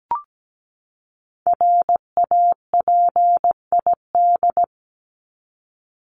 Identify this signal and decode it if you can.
Morse code, heard as a keyed tone, about 17 wpm, 700 Hz. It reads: RAPID
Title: LY Suffix - Top 100 Words with Base Words 17wpm